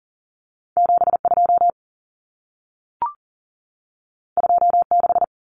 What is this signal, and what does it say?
Morse code, heard as a keyed tone, about 40 wpm, 700 Hz. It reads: 72 26
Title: Numbers - 2-Digit 40wpm